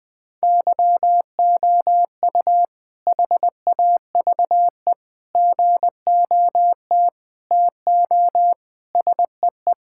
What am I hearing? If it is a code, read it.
Morse code, heard as a keyed tone, about 21 wpm, 700 Hz. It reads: YOU HAVE GOT TO SEE
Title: Sentences with 2-Word Phrases - Spoken Beginning - 20wpm